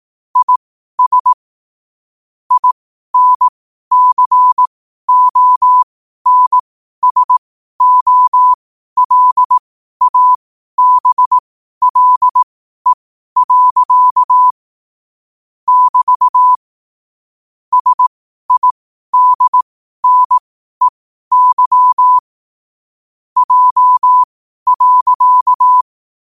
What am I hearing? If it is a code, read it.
Morse code, heard as a keyed tone, about 18 wpm, spaced as for 12 wpm, 1000 Hz. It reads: IS INCONSOLABLE. <BT> SIDNEY J.
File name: qotdfiles / 12wpm